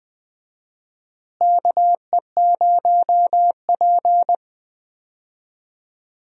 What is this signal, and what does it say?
Morse code, heard as a keyed tone, about 20 wpm, 700 Hz. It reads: KE0P
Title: USA Call Signs - 4 Characters - Repeated 2x - 20wpm